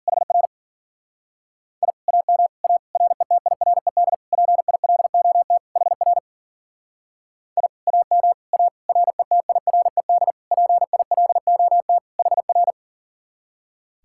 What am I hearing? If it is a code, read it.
Morse code, heard as a keyed tone, about 48 wpm, 700 Hz. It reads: HR I AM A RETIRED PILOT HR I AM A RETIRED PILOT HR
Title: Speed-Racing Common QSO Phrases - 40wpm